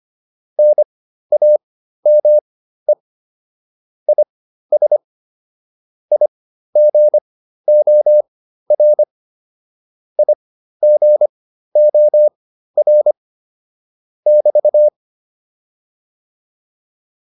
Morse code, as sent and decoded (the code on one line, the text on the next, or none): -. .- -- . / .. ... / .. --. --- .-. / .. --. --- .-. / -...-
NAME IS IGOR IGOR <BT>